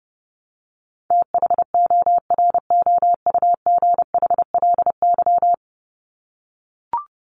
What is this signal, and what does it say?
Morse code, heard as a keyed tone, about 30 wpm, 700 Hz. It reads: THOROUGHLY
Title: LY Suffix - Top 100 Words with Base Words 30wpm